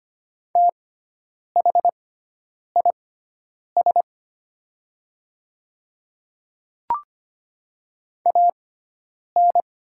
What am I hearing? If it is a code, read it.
Morse code, heard as a keyed tone, about 25 wpm, spaced as for 9 wpm, 700 Hz. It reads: THIS AN
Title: 6x Character Spacing - Top 200 Words - 25wpm